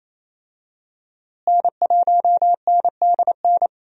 Morse code, heard as a keyed tone, about 28 wpm, 700 Hz. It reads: N1NDN
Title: USA Call Signs - 5 Characters - Repeated 2x - 28wpm